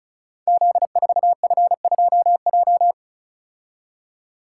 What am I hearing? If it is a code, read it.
Morse code, heard as a keyed tone, about 36 wpm, 700 Hz. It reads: Z4F2J
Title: Code Groups - 5 Letter:Numbers - 35wpm